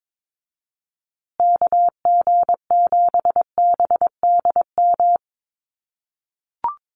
Morse code, heard as a keyed tone, about 22 wpm, 700 Hz. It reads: KG7BDM
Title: USA Call Signs Repeated 2x 22wpm